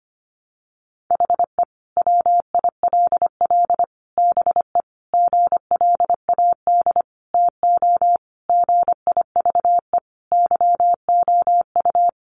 Morse code, heard as a keyed tone, about 25 wpm, 700 Hz. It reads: HE WILL BE GLAD TO GIVE YOU
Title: Sentences from Top 800 Words 25wpm